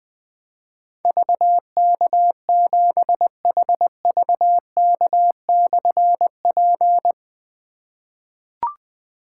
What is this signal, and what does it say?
Morse code, heard as a keyed tone, about 20 wpm, 700 Hz. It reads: VK7HVK/P
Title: Difficult Call Signs 20wpm